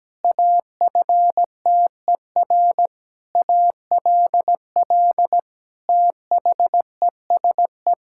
Morse code, heard as a keyed tone, about 17 wpm, 700 Hz. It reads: AFTER ALL THESE
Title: Sentences from Top 1000 Words - Review Part 6 - 17wpm